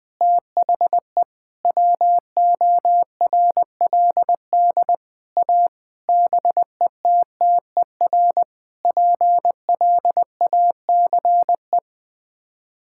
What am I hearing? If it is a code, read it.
Morse code, heard as a keyed tone, about 20 wpm, 700 Hz. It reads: THE WORLD A BETTER PLACE
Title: Sets of 6 Words - 20wpm